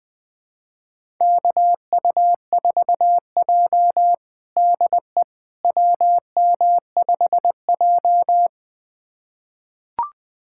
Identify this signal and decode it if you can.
Morse code, heard as a keyed tone, about 20 wpm, 700 Hz. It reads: KU4J DE WM5J
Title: USA Callsign DE Callsign - 4 Characters - 20wpm